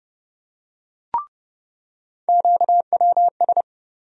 Morse code, heard as a keyed tone, about 30 wpm, 700 Hz. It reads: QWS
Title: USA Call Sign Postfixes - 3 Letters 30wpm